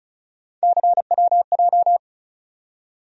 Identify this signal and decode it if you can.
Morse code, heard as a keyed tone, about 34 wpm, 700 Hz. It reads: CWJ
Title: USA Call Sign Postfixes - 3 Letters 35wpm